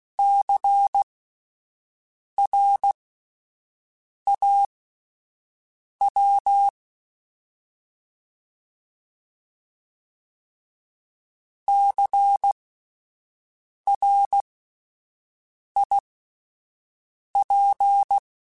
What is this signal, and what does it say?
Morse code, heard as a keyed tone, about 16 wpm, spaced as for 5 wpm, 800 Hz. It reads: CRAW CRIP
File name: Spaced Alphas / Spaced Alphas mp3s / Cs sp @10